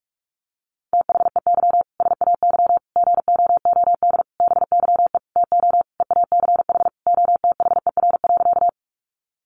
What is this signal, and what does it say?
Morse code, heard as a keyed tone, about 44 wpm, 700 Hz. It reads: THEY SAY GOOD BYE TO EACH OTHER.
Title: Easy Phrases 45wpm